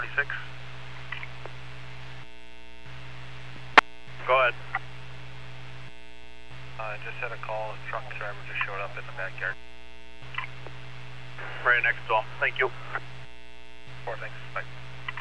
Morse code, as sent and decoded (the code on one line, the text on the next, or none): none